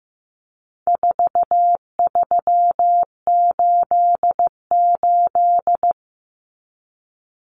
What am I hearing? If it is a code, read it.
Morse code, heard as a keyed tone, about 15 wpm, 700 Hz. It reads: 4388
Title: Common Cut Numbers - 4-Digit 15wpm